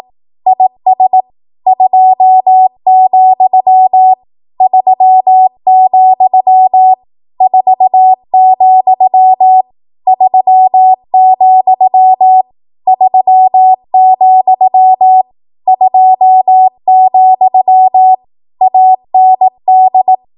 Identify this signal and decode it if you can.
Morse code, heard as a keyed tone, about 18 wpm, 750 Hz. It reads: IS 2, 3, 4, 3, 3, 2, AND
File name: Morse / 18WPM